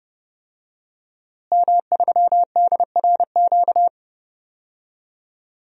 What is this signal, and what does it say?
Morse code, heard as a keyed tone, about 30 wpm, 700 Hz. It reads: M3DRQ
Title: DXCC-100 Call Signs - 5 Characters - Repeated 2x - 30wpm